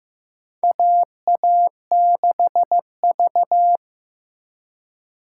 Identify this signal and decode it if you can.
Morse code, heard as a keyed tone, about 15 wpm, 700 Hz. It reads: AA6V
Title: USA Call Signs - 4 Characters - 2x Character Spacing - 15wpm